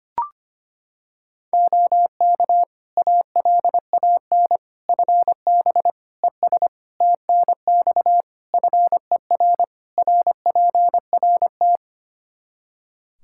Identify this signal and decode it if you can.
Morse code, heard as a keyed tone, about 25 wpm, 700 Hz. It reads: OK ALAN FB ES TNX FER RPRT
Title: Practice QSO - Exchange 2 - Info - 25wpm